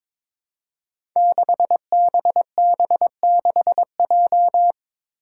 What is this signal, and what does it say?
Morse code, heard as a keyed tone, about 22 wpm, 700 Hz. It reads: 6BB6J
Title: Five Challenging Letter-Number Combinations 22wpm